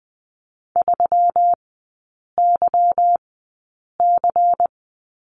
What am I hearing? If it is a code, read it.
Morse code, heard as a keyed tone, about 20 wpm, spaced as for 8 wpm, 700 Hz. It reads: 3YC
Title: 205 Lesson 33e - Letter j - Three Characters 20wpm